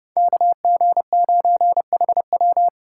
Morse code, heard as a keyed tone, about 34 wpm, 700 Hz. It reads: KG9HW
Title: Speed Racing - USA 2x2 Call Signs 30wpm